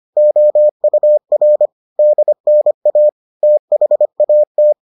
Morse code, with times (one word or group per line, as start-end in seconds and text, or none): 0.2-1.7 OUR
2.0-3.1 DNA
3.4-4.7 THAT